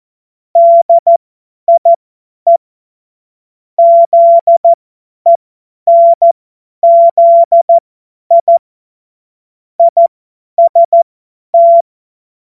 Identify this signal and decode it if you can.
Morse code, heard as a keyed tone, about 14 wpm, spaced as for 10 wpm, 680 Hz. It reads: DIE ZENZI IST
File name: Morsekurs / Witze / Witz 04. April 2023 50 F70